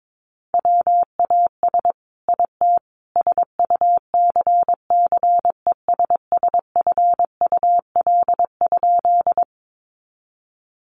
Morse code, heard as a keyed tone, about 22 wpm, 700 Hz. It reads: WAS IT SUCCESSFUL?
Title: A Sound Of Thunder 22wpm